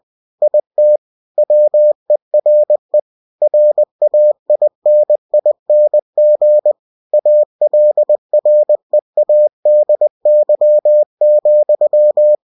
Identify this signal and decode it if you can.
Morse code, heard as a keyed tone, about 20 wpm, 600 Hz. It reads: IT WERE RAINING ALREADY,
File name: cw-books / winnie-the-pooh / winnie-20wpm-page-0035